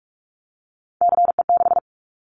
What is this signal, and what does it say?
Morse code, heard as a keyed tone, about 44 wpm, 700 Hz. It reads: CE6
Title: Callsign Prefixes for Top 100 DXCC - 45wpm